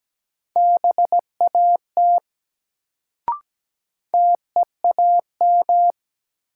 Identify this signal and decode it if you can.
Morse code, heard as a keyed tone, about 17 wpm, 700 Hz. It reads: BAT TEAM
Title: Sentences from Top 1500 Words - New Word Drill Alt - 17wpm